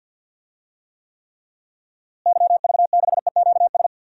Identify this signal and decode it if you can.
Morse code, heard as a keyed tone, about 52 wpm, 700 Hz. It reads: YV6EQS
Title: DXCC-100 Call Signs - 6 Characters - Repeated 3x - 50wpm